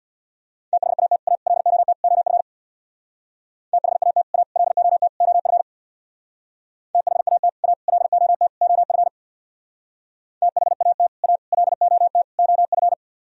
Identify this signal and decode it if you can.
Morse code, heard as a keyed tone, about 63 wpm, 700 Hz. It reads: THAT A LOT OF THAT A LOT OF THAT A LOT OF THAT A LOT OF